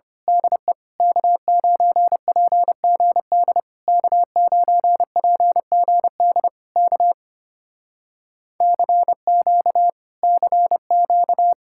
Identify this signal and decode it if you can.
Morse code, heard as a keyed tone, about 25 wpm, 700 Hz. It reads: DE K9PGD K9PGD K CQ CQ
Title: Speed-Racing Common QSO Phrases - 25wpm